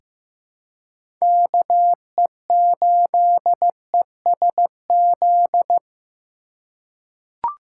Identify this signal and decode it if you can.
Morse code, heard as a keyed tone, about 15 wpm, 700 Hz. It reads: KE8ESZ
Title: USA Call Signs - 6 Characters - 2x Character Spacing - 15wpm